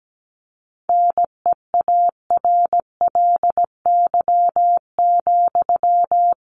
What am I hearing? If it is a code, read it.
Morse code, heard as a keyed tone, about 17 wpm, 700 Hz. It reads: NEARLY,
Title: Sentences from Top 1300 Words 17wpm